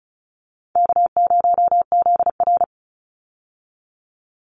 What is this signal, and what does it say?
Morse code, heard as a keyed tone, about 34 wpm, 700 Hz. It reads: K0ZR
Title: Easy Call Signs Repeated 2x - 35wpm